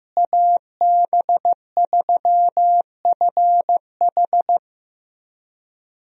Morse code, heard as a keyed tone, about 15 wpm, 700 Hz. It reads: AB3FH
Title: USA Call Signs - 5 Characters - Repeated 2x - 15wpm